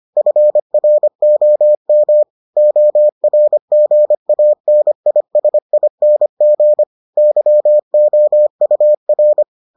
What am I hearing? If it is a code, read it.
Morse code, heard as a keyed tone, about 25 wpm, 600 Hz. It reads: FROM ORGANISING YOUR